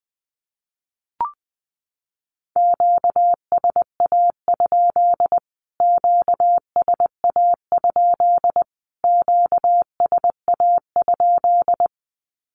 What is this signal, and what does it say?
Morse code, heard as a keyed tone, about 20 wpm, 700 Hz. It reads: QSA? QSA? QSA?
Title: Q-Codes Repeated 3x 20wpm